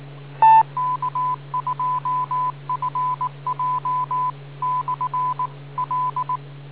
Morse code, heard as a keyed tone, about 19 wpm, 1000 Hz. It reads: K2FJ/L